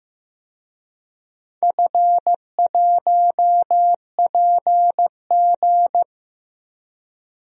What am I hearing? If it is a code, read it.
Morse code, heard as a keyed tone, about 16 wpm, 700 Hz. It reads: F1PG
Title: DXCC-100 Call Signs - 4 Characters - Repeated 2x - 15wpm